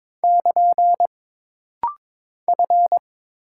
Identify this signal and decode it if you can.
Morse code, heard as a keyed tone, about 22 wpm, 700 Hz. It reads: <KN> F
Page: This rapid-fire timing is geared to push your ICR (Instant Character Recognition) skills.